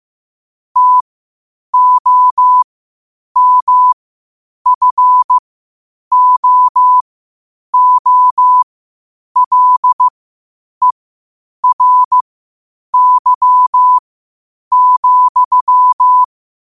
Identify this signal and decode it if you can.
Morse code, heard as a keyed tone, about 15 wpm, spaced as for 8 wpm, 1000 Hz. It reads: TOMFOOLERY,
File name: qotdfiles / 8wpm